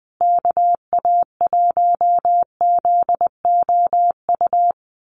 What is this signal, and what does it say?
Morse code, heard as a keyed tone, about 20 wpm, 700 Hz. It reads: KA1ZOU
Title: USA Call Signs Repeated 2x 20wpm